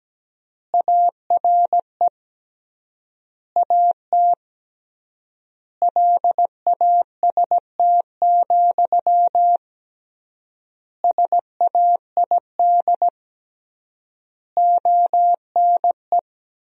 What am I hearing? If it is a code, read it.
Morse code, heard as a keyed tone, about 17 wpm, 700 Hz. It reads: ARE AT LAST, SAID ONE
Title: 3x Word-Spacing - Sentences from Top 300 Words - 17wpm